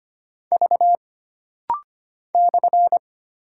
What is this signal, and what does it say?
Morse code, heard as a keyed tone, about 25 wpm, 700 Hz. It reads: V /
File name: Single Character (Letter:Number:Punctuation:Prosign) - Rapid-Fire - 25wpm